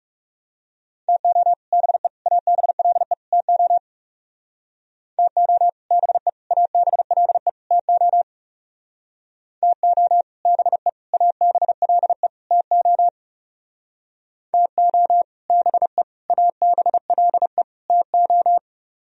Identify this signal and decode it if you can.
Morse code, heard as a keyed tone, about 37 wpm, 700 Hz. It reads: TO BE ABLE TO TO BE ABLE TO TO BE ABLE TO TO BE ABLE TO
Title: Speed-Racing - Sets of 4 Words 30wpm